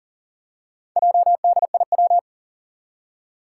Morse code, heard as a keyed tone, about 37 wpm, 700 Hz. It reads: JDIW
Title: Four Challenging Letter-Number Combinations 40wpm